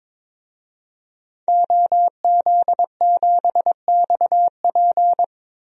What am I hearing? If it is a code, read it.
Morse code, heard as a keyed tone, about 22 wpm, 700 Hz. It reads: OZ7XP
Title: DXCC-100 Call Signs - 5 Characters - Repeated 2x - 22wpm